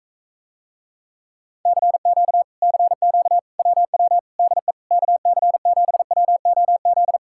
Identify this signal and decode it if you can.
Morse code, heard as a keyed tone, about 45 wpm, 700 Hz. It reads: CQ CQ WW DE KC7WOZ